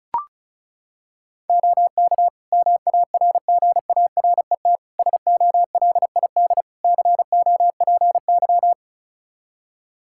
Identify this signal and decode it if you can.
Morse code, heard as a keyed tone, about 36 wpm, 700 Hz. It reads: OK MARGARET SOLID COPY
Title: Practice QSO - Exchange 3 - More Info - 35wpm